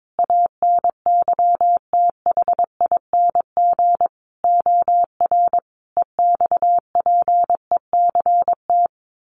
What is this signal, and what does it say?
Morse code, heard as a keyed tone, about 22 wpm, 700 Hz. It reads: ANYTHING OR EXPECT
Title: Sentences from Top 1400 Words 22wpm